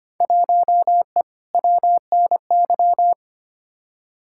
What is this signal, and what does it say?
Morse code, heard as a keyed tone, about 25 wpm, 700 Hz. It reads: 1E WNY
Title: Field Day Exchanges - 25wpm